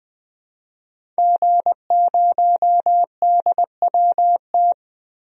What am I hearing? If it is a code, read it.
Morse code, heard as a keyed tone, about 20 wpm, 700 Hz. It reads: G0DWT
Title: DXCC-100 Call Signs - 5 Characters - Repeated 2x - 20wpm